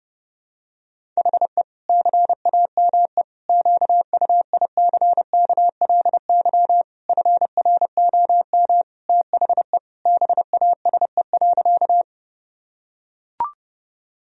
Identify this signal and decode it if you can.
Morse code, heard as a keyed tone, about 30 wpm, 700 Hz. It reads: HE CAME QUICKLY FROM THE BASE.
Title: Sentences from Top 1000 Words - Review Part 1 - 30wpm